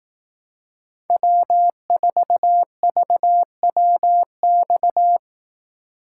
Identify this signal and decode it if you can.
Morse code, heard as a keyed tone, about 18 wpm, 700 Hz. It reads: W4VWX
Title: Speed Racing - USA 1x3 Call Signs 15wpm